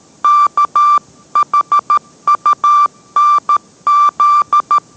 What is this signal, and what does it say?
Morse code, heard as a keyed tone, about 13 wpm, 1180 Hz. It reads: KHUNZ